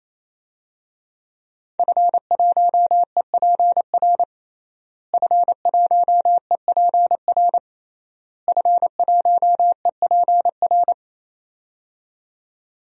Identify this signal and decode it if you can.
Morse code, heard as a keyed tone, about 28 wpm, 700 Hz. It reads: F1EPR F1EPR F1EPR
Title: DXCC-100 Call Signs - 5 Characters - Repeated 3x - 28wpm